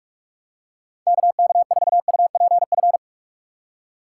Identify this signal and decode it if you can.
Morse code, heard as a keyed tone, about 46 wpm, 700 Hz. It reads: KK4UPF
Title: USA Call Signs - 6 Characters - 2x Character Spacing - 45wpm